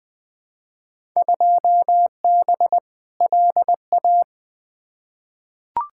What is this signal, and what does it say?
Morse code, heard as a keyed tone, about 20 wpm, 700 Hz. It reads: 2B LA
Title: Field Day Exchanges Encore - 20wpm